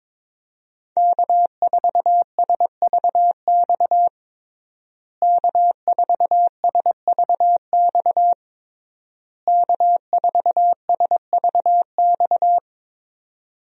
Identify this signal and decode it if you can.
Morse code, heard as a keyed tone, about 22 wpm, 700 Hz. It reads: K4SVX K4SVX K4SVX
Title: USA Call Signs - 5 Characters - Repeated 3x - 22wpm